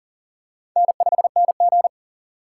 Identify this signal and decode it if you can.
Morse code, heard as a keyed tone, about 44 wpm, 700 Hz. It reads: N5NG
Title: USA Call Signs - 4 Characters - 3x Character Spacing - 40wpm